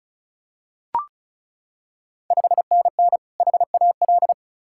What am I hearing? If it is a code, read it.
Morse code, heard as a keyed tone, about 36 wpm, 690 Hz. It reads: 5NN HAL
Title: Ontario QSO Party Exchange 35wpm